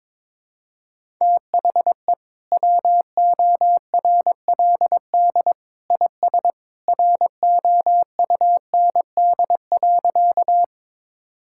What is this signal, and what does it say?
Morse code, heard as a keyed tone, about 22 wpm, 700 Hz. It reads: THE WORLD IS ROUND.
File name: practice / 22wpm / Sentences with 2-Word Phrases - 22wpm